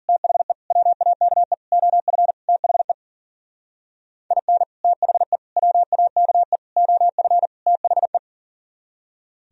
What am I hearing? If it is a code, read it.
Morse code, heard as a keyed tone, about 50 wpm, 700 Hz. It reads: THE WAKE OF THE IN THE WAKE OF THE